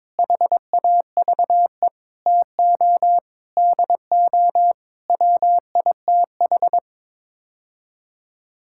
Morse code, heard as a keyed tone, about 22 wpm, 700 Hz. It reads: HAVE TO DO WITH
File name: practice / 22wpm / Sets of 8 Words - 22wpm